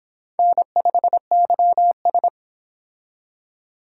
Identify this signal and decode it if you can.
Morse code, heard as a keyed tone, about 32 wpm, 700 Hz. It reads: N5YS